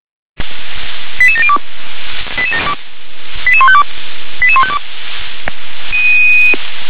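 Five-tone selective call (ZVEI), heard as quick runs of five tones, tones of about 70 ms. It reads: 88702 88152 88152
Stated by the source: ZVEI five-tone selective call 88702, received by SDR.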